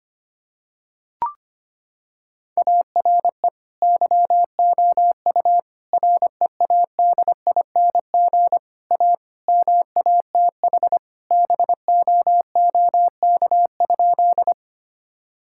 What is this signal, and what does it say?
Morse code, heard as a keyed tone, about 25 wpm, 700 Hz. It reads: ARE YOU READING A MATH BOOK?
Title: Sentences from Top 1000 Words - Review Part 6 - 25wpm